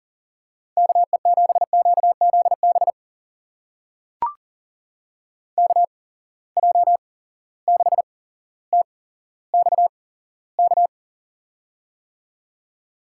Morse code, heard as a keyed tone, about 39 wpm, spaced as for 17 wpm, 700 Hz. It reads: KE7QZB KJ6TXK
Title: USA Call Signs - 6 Characters - 8x Character Spacing - 40wpm